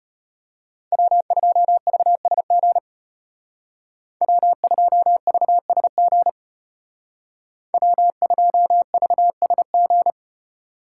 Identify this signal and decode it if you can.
Morse code, heard as a keyed tone, about 33 wpm, 700 Hz. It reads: W2VSG W2VSG W2VSG